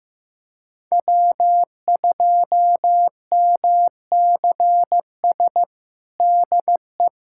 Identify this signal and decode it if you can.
Morse code, heard as a keyed tone, about 15 wpm, 700 Hz. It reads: W2MCS DE
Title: USA Callsign DE Callsign - 5 Characters - 15wpm